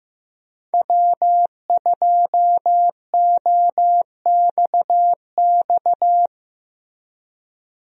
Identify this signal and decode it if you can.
Morse code, heard as a keyed tone, about 15 wpm, 700 Hz. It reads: W2OXX
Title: USA Call Signs - 5 Characters - Repeated 2x - 15wpm